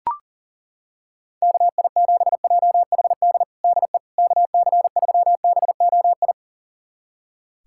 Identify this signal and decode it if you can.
Morse code, heard as a keyed tone, about 40 wpm, 700 Hz. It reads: KI7JHD DE KC3BOI
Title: USA Callsign DE Callsign 40wpm